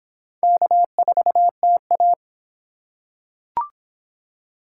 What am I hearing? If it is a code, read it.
Morse code, heard as a keyed tone, about 23 wpm, 700 Hz. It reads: K4TA